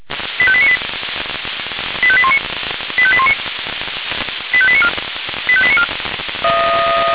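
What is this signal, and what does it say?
Five-tone selective call (ZVEI), heard as quick runs of five tones, tones of about 70 ms. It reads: 85898 85819 85819 85894 85894
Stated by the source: ZVEI five-tone selective call received by SDR.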